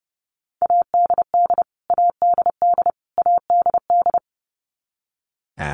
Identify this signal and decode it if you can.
Morse code, heard as a keyed tone, about 30 wpm, 700 Hz. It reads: ADD ADD ADD
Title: Top 200 Words Repeated 30wpm